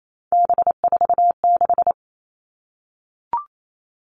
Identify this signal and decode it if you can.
Morse code, heard as a keyed tone, about 28 wpm, 700 Hz. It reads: B46
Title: All Cut Numbers - 3-Digit 28wpm